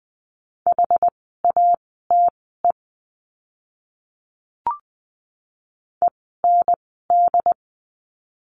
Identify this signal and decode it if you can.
Morse code, heard as a keyed tone, about 20 wpm, spaced as for 14 wpm, 700 Hz. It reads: HATE END
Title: 068 Lesson 12f - Letter d - Words 20wpm